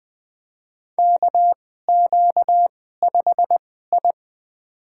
Morse code, heard as a keyed tone, about 20 wpm, spaced as for 14 wpm, 700 Hz. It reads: KQ5I
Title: USA Call Signs - 4 Characters - 2x Character Spacing - 20wpm